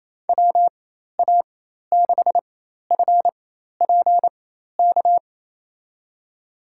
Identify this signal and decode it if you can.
Morse code, heard as a keyed tone, about 28 wpm, spaced as for 13 wpm, 700 Hz. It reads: WA6FPK
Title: USA Call Signs - 6 Characters - 4x Character Spacing - 28wpm